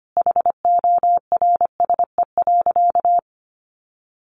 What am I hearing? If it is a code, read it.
Morse code, heard as a keyed tone, about 25 wpm, 700 Hz. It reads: HORSE.
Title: Sentences from Top 1100 Words 25wpm